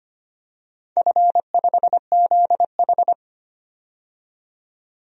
Morse code, heard as a keyed tone, about 25 wpm, 700 Hz. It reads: F5ZH
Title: DXCC-100 Call Signs - 4 Characters - 25wpm